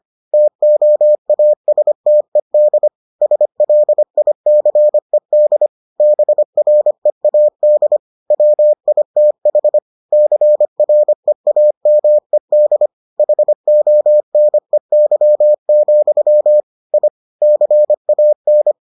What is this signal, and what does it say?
Morse code, heard as a keyed tone, about 25 wpm, 600 Hz. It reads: TOASTED SLICED BREAD WITH CREAMED HONEY, I CAN